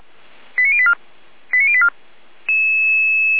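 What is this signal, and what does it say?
Five-tone selective call (ZVEI), heard as quick runs of five tones, tones of about 70 ms. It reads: 89084 89084
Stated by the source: ZVEI five-tone selective call received on an SDR.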